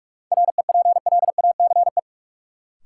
Morse code, heard as a keyed tone, about 46 wpm, 700 Hz. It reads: REPLACE
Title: Sentences from Top 1400 Words - New Word Drill Alt - 45wpm